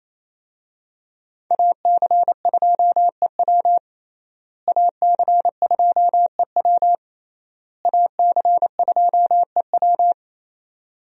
Morse code, heard as a keyed tone, about 27 wpm, 700 Hz. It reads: AC2EW AC2EW AC2EW
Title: USA Call Signs - 5 Characters - Repeated 3x - 28wpm